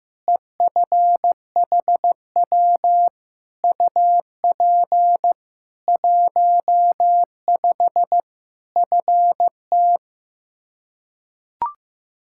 Morse code, heard as a keyed tone, about 15 wpm, 700 Hz. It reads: EFHW UP 15 FT
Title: K1USN SST QSO Practice 15wpm